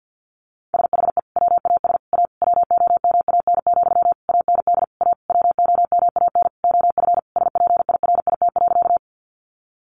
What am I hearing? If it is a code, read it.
Morse code, heard as a keyed tone, about 48 wpm, 700 Hz. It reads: SHE WAS A WOMAN, AND A WOMAN OF SPIRIT.
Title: Sentences from Top 1400 Words 50wpm